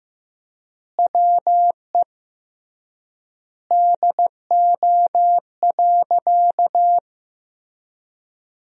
Morse code, heard as a keyed tone, about 15 wpm, 700 Hz. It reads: WE DO.
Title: 3x Word-Spacing - Sentences from Top 100 Words - 15wpm